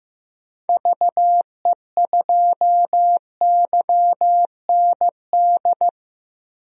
Morse code, heard as a keyed tone, about 15 wpm, 700 Hz. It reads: VE2YND
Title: DXCC-100 Call Signs - 6 Characters - Repeated 3x - 15wpm